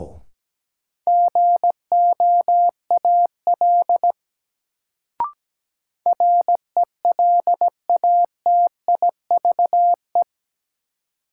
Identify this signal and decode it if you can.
Morse code, heard as a keyed tone, about 17 wpm, 700 Hz. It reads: GOAL RELATIVE
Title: Sentences from Top 1400 Words - New Word Drill Alt - 17wpm